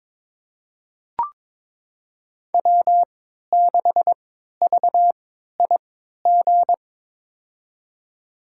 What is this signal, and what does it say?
Morse code, heard as a keyed tone, about 21 wpm, spaced as for 12 wpm, 700 Hz. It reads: W6VIG
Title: USA Call Signs - 5 Characters - 3x Character Spacing - 22wpm